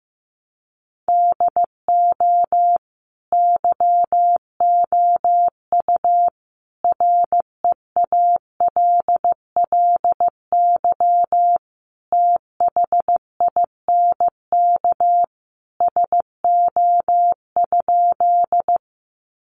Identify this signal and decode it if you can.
Morse code, heard as a keyed tone, about 15 wpm, 700 Hz. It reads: DO YOU REALLY THINK SO?
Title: Sentences from Top 600 Words 15wpm